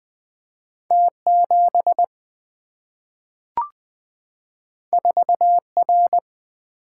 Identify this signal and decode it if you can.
Morse code, heard as a keyed tone, about 20 wpm, 700 Hz. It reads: T7 4R
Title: Single-Letter Single-Number 20wpm